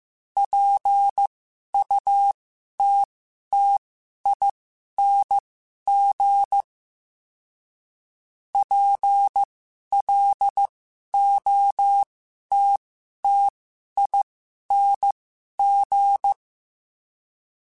Morse code, heard as a keyed tone, about 15 wpm, spaced as for 11 wpm, 800 Hz. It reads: PUTTING PLOTTING
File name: Special INGs-mp3s / Special INGs-mp3s / Sngl Ing @10